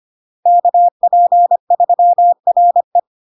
Morse code, heard as a keyed tone, about 25 wpm, 700 Hz. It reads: KP3RE
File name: Log-in-CW-25wpm-700-Hz